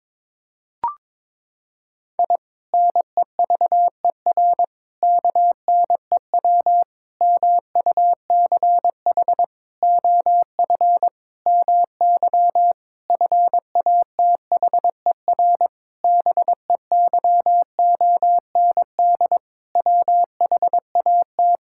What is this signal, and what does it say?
Morse code, heard as a keyed tone, about 22 wpm, 700 Hz. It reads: I NEVER KNEW MUCH OF MY FATHER BEYOND WHAT
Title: Sentences from Top 1000 Words - Review Part 6 - 22wpm